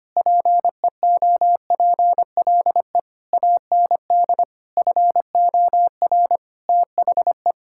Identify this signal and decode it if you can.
Morse code, heard as a keyed tone, about 25 wpm, 700 Hz. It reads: PEOPLE AND FOR THE
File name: practice / 25wpm / Sentences with 9-Word Phrases - 25wpm